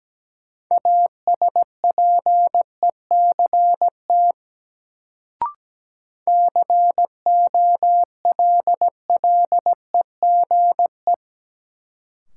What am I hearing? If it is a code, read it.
Morse code, heard as a keyed tone, about 17 wpm, 700 Hz. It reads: ASPECT COLLEGE
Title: Sentences from Top 1300 Words - New Word Drill - 17wpm